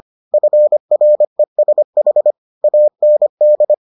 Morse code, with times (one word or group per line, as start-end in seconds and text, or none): 0.3-2.3 FRESH
2.6-3.7 AND